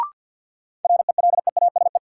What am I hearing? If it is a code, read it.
Morse code, heard as a keyed tone, about 47 wpm, 700 Hz. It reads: RELEASE
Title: Sentences from Top 1100 Words - New Word Drill Alt - 50wpm